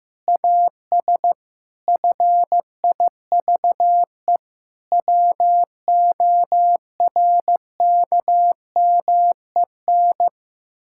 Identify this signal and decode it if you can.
Morse code, heard as a keyed tone, about 15 wpm, 700 Hz. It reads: AS FIVE WORKMEN
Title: Pangram Challenge v2 - 15wpm